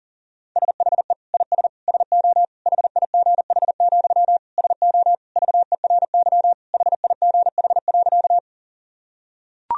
Morse code, heard as a keyed tone, about 40 wpm, 700 Hz. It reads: SHE IS SO HIGH, SO VERY HIGH.
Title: Sentences from Top 1000 Words - Review Part 1 - 40wpm